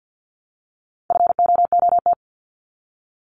Morse code, heard as a keyed tone, about 53 wpm, 700 Hz. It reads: FOOT
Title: Top 500 Words Repeated 50wpm